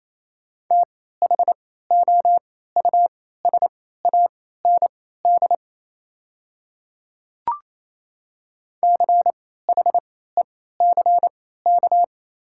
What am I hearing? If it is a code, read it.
Morse code, heard as a keyed tone, about 28 wpm, spaced as for 16 wpm, 700 Hz. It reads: THOUSAND CHECK
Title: 3x Character Spacing - Top 500 Words - 28wpm